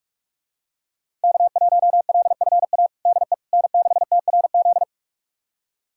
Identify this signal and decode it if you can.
Morse code, heard as a keyed tone, about 46 wpm, 700 Hz. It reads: K1LFA DE N6TRZ
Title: USA Callsign DE Callsign - 5 Characters - 45wpm